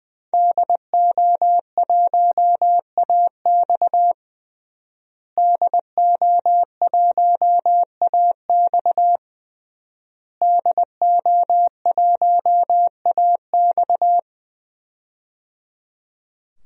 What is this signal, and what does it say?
Morse code, heard as a keyed tone, about 20 wpm, 700 Hz. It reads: DO1AX DO1AX DO1AX
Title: DXCC-100 Call Signs - 5 Characters - Repeated 3x - 20wpm